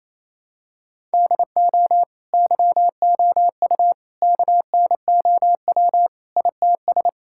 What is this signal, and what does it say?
Morse code, heard as a keyed tone, about 27 wpm, 700 Hz. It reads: DO YOU KNOW ITS
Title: Sentences from Top 1000 Words - Review Part 5 - 28wpm